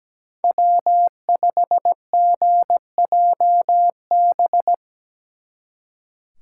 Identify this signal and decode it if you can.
Morse code, heard as a keyed tone, about 20 wpm, 700 Hz. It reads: W5GJB
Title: Speed Racing - USA 1x3 Call Signs 17wpm